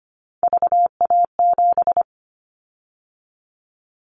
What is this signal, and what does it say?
Morse code, heard as a keyed tone, about 25 wpm, 700 Hz. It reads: VA7
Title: Callsign Prefixes for Top 50 DXCC - 25wpm